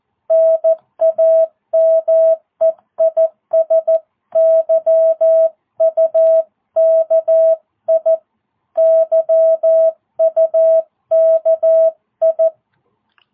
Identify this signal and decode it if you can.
Morse code, heard as a keyed tone, about 14 wpm, 660 Hz. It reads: NAME IS YUKI YUKI